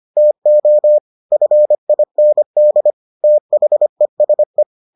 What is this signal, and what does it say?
Morse code, heard as a keyed tone, about 25 wpm, 600 Hz. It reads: TO FIND THESE